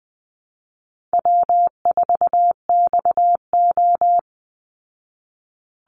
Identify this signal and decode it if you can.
Morse code, heard as a keyed tone, about 20 wpm, 700 Hz. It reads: W4XO
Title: Easy Call Signs Repeated 2x - 20wpm